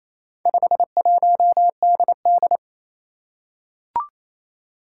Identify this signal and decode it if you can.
Morse code, heard as a keyed tone, about 28 wpm, 700 Hz. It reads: 51DD
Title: Four Challenging Letter-Number Combinations 28wpm